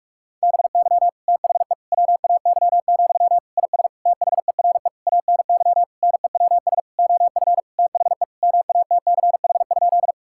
Morse code, heard as a keyed tone, about 41 wpm, 700 Hz. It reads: BY THE WAY, IS THERE ANY NEWS OF THE MATCH?
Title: Sentences from Top 1600 Words 45wpm